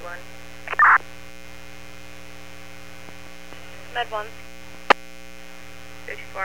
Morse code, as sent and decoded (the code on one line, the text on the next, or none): none